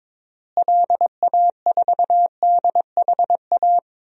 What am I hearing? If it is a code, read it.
Morse code, heard as a keyed tone, about 22 wpm, 700 Hz. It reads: LA4DHA
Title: DXCC-100 Call Signs - 6 Characters - Repeated 3x - 22wpm